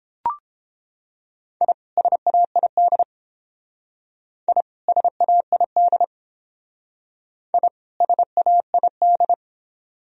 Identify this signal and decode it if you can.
Morse code, heard as a keyed tone, about 28 wpm, 700 Hz. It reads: I SAID I SAID I SAID